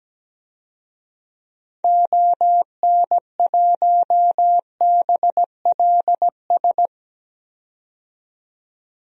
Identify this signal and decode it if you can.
Morse code, heard as a keyed tone, about 17 wpm, 700 Hz. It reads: ON1BLS
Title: DXCC-100 Call Signs - 6 Characters - 17wpm